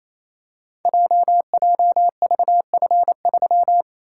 Morse code, heard as a keyed tone, about 28 wpm, 700 Hz. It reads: JJVF3
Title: Code Groups - 5 Letter:Numbers - 28wpm